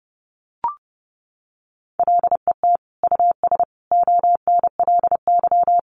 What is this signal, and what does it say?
Morse code, heard as a keyed tone, about 30 wpm, 700 Hz. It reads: LET US ONLY
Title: Sentences from Top 800 Words 30wpm